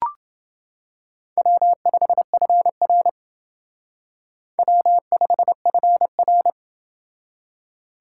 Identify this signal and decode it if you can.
Morse code, heard as a keyed tone, about 29 wpm, 700 Hz. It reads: W5FR W5FR